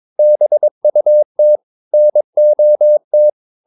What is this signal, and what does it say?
Morse code, heard as a keyed tone, about 22 wpm, 600 Hz. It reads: BUT NOT
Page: Romanen Never af Ken Follet på engelsk.